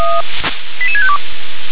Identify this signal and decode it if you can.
Five-tone selective call (ZVEI), heard as quick runs of five tones, tones of about 70 ms. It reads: 88652